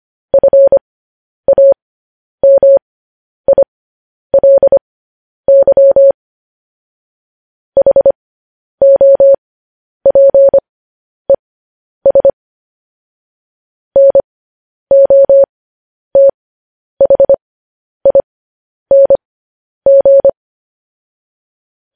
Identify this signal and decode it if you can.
Morse code, heard as a keyed tone, about 25 wpm, spaced as for 10 wpm, 550 Hz. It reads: FAMILY HOPES NOTHING
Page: news-headlines-in-morse-code-at-10-wpm